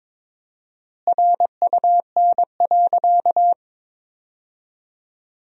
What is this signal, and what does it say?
Morse code, heard as a keyed tone, about 22 wpm, 700 Hz. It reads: RUN.
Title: 4x Word-Spacing - Sentences from Top 500 Words - 22wpm